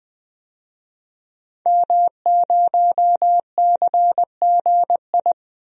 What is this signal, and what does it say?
Morse code, heard as a keyed tone, about 20 wpm, 700 Hz. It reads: M0CGI
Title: DXCC-100 Call Signs - 5 Characters - 20wpm